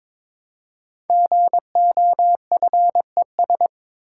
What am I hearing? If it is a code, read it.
Morse code, heard as a keyed tone, about 22 wpm, 700 Hz. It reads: GOFES
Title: Code Groups - 5 Letters - 22wpm